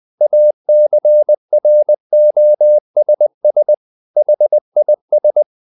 Morse code, heard as a keyed tone, about 20 wpm, 600 Hz. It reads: ACROSS HIS
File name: the old man and the sea part 2 20wpm